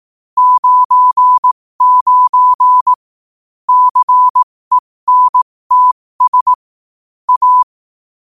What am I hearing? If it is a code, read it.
Morse code, heard as a keyed tone, about 18 wpm, spaced as for 15 wpm, 1000 Hz. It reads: 99 CENTS A
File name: qotdfiles / 15wpm